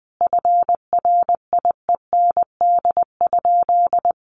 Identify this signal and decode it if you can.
Morse code, heard as a keyed tone, about 20 wpm, 700 Hz. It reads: FRIEND?
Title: Sentences from Top 1000 Words 20wpm